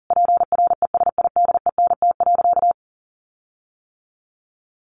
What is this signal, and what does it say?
Morse code, heard as a keyed tone, about 40 wpm, 700 Hz. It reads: PRESIDENT.
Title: Sentences from Top 1500 Words 40wpm